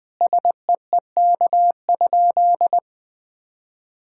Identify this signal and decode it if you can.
Morse code, heard as a keyed tone, about 20 wpm, 700 Hz. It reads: SEEK?
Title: Sentences from Top 1600 Words 20wpm